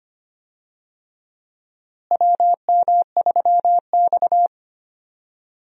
Morse code, heard as a keyed tone, about 25 wpm, 700 Hz. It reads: WM3X
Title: USA Call Signs - 4 Characters - Repeated 3x - 25wpm